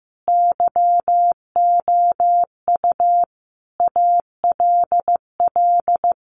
Morse code, heard as a keyed tone, about 15 wpm, 700 Hz. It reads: YOU ALL
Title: Sets of 5 Words Encore 15wpm